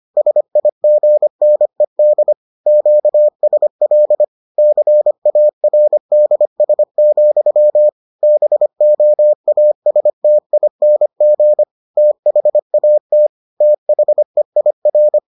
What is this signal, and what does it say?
Morse code, heard as a keyed tone, about 25 wpm, 600 Hz. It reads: SIGNED QSL CARDS, BOASTING THAT THEIR